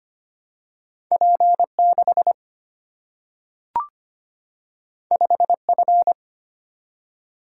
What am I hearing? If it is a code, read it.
Morse code, heard as a keyed tone, about 25 wpm, 700 Hz. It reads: P6 5F
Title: Single-Letter Single-Number 25wpm